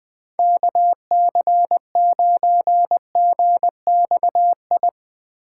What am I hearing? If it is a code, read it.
Morse code, heard as a keyed tone, about 20 wpm, 700 Hz. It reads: KC9GXI
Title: Speed Racing - USA Call Signs 17wpm